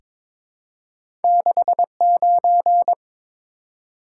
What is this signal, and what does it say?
Morse code, heard as a keyed tone, about 22 wpm, 700 Hz. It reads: 69
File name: practice / 22wpm / Two Letter-Number Combinations - 22wpm